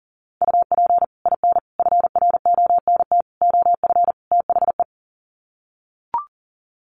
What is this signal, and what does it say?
Morse code, heard as a keyed tone, about 40 wpm, 700 Hz. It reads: UP IN FRONT OF THE
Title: Sets of 5 Words Encore2 40wpm